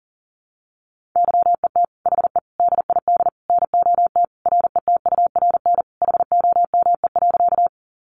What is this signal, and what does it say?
Morse code, heard as a keyed tone, about 40 wpm, 700 Hz. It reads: YET HE DID NOT RETURN HOME.
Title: Sentences from Top 800 Words 40wpm